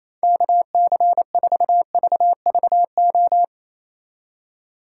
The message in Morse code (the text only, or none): KC4VVO